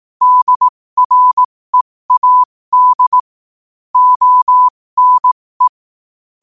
Morse code, heard as a keyed tone, about 18 wpm, spaced as for 15 wpm, 1000 Hz. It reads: DREAD ONE
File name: qotdfiles / 15wpm